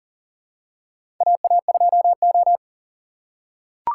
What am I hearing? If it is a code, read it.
Morse code, heard as a keyed tone, about 39 wpm, 700 Hz. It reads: AA2O
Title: USA Call Signs - 3x Character Spacing - 40wpm